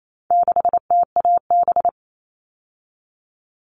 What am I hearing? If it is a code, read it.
Morse code, heard as a keyed tone, about 28 wpm, 700 Hz. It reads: 6TAB
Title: All Cut Numbers - 4-Digit 28wpm